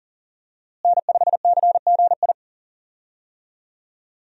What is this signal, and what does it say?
Morse code, heard as a keyed tone, about 38 wpm, 700 Hz. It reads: N5CGI
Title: USA Call Signs - 5 Characters - Repeated 2x - 40wpm